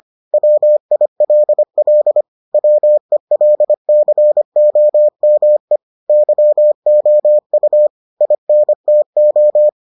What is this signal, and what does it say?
Morse code, heard as a keyed tone, about 25 wpm, 600 Hz. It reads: WILL WELCOME YOU INTO